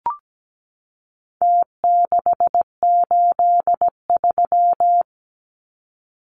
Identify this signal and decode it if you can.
Morse code, heard as a keyed tone, about 17 wpm, 700 Hz. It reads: T683
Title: Numbers - Contest Style - 4-Digit 17wpm